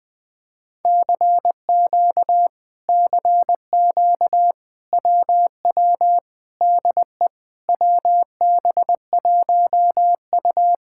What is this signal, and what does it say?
Morse code, heard as a keyed tone, about 20 wpm, 700 Hz. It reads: CQ CQ WW DE WB1U
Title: Speed-Racing Common QSO Phrases - 20wpm